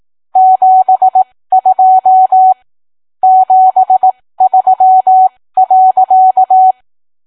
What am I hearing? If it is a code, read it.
Morse code, heard as a keyed tone, about 18 wpm, spaced as for 15 wpm, 750 Hz. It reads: 72 73.